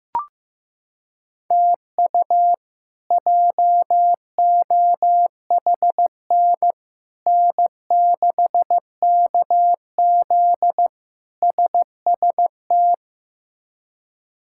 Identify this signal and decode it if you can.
Morse code, heard as a keyed tone, about 15 wpm, 700 Hz. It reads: TU JOHN N6KZ SST
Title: K1USN SST QSO Practice - 15wpm